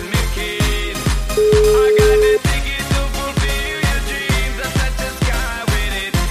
Ringback tone (Europe, 425 Hz), with noise.